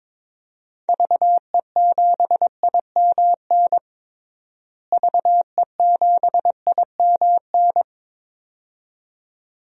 Morse code, heard as a keyed tone, about 22 wpm, 700 Hz. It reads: VE7IMN VE7IMN
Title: DXCC-100 Call Signs - 6 Characters - Repeated 2x - 22wpm